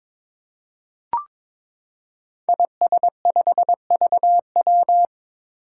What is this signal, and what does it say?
Morse code, heard as a keyed tone, about 22 wpm, 700 Hz. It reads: IS5VW
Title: Five Challenging Letter-Number Combinations 22wpm